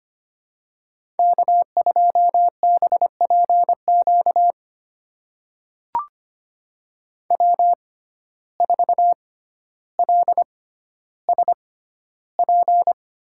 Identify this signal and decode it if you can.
Morse code, heard as a keyed tone, about 25 wpm, spaced as for 13 wpm, 700 Hz. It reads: K2BPQ W4LSP
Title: USA Call Signs - 5 Characters - 6x Character Spacing - 25wpm